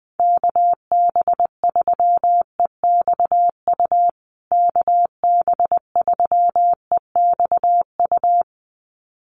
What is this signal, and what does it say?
Morse code, heard as a keyed tone, about 20 wpm, 700 Hz. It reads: KB3EXU KB3EXU
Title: USA Call Signs Repeated 2x 20wpm